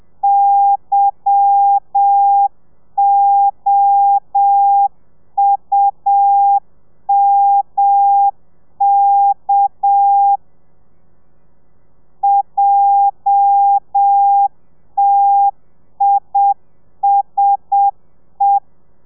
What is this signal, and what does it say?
Morse code, heard as a keyed tone, about 7 wpm, 800 Hz. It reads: YOUMK JTISE
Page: Lezione 1: audio a 30 CPM